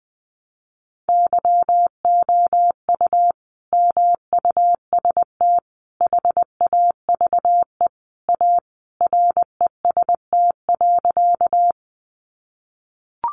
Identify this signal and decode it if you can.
Morse code, heard as a keyed tone, about 20 wpm, 700 Hz. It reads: YOU MUST HAVE A REST.
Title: Sentences from Top 500 Words 20wpm